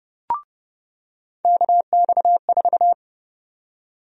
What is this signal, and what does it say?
Morse code, heard as a keyed tone, about 30 wpm, 700 Hz. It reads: KX4
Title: USA Call Sign Prefixes - 30wpm